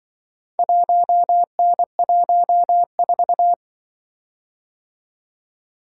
Morse code, heard as a keyed tone, about 24 wpm, 700 Hz. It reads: 1N14
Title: Speed-Racing - Four Letter-Number Combinations - 20wpm